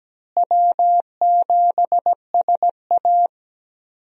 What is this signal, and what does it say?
Morse code, heard as a keyed tone, about 17 wpm, 700 Hz. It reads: W7SA
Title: USA Call Signs - 4 Characters - Repeated 2x - 17wpm